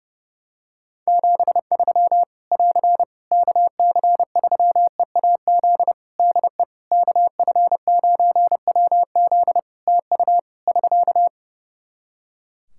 Morse code, heard as a keyed tone, about 30 wpm, 700 Hz. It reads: 73 <AR> KC3EAZ DE KF9WZ TU <SK>
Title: Practice QSO - Exchange 4 - Ending - 30wpm